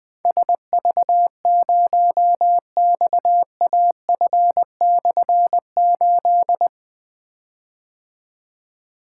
Morse code, heard as a keyed tone, about 20 wpm, 700 Hz. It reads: SV0XAF/8
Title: Difficult Call Signs 20wpm